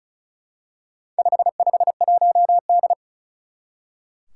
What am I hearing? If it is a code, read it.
Morse code, heard as a keyed tone, about 36 wpm, 700 Hz. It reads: 551D
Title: Four Challenging Letter-Number Combinations 35wpm